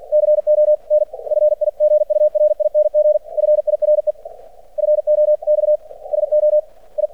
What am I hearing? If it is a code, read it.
Morse code, heard as a keyed tone, about 50 wpm, spaced as for 38 wpm, 600 Hz. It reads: CONFIGURING FIREFOX SO I
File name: archived / CRSnet